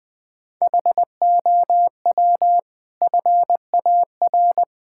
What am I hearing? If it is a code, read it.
Morse code, heard as a keyed tone, about 21 wpm, 700 Hz. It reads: HOW FAR
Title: Sentences from Top 1600 Words 20wpm